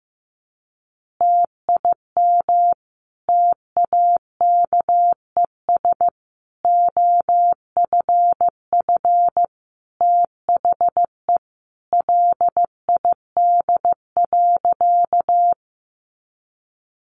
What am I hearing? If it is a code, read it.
Morse code, heard as a keyed tone, about 15 wpm, 700 Hz. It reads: TIM TAKES OFF THE LID.
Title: Easy Phrases 15wpm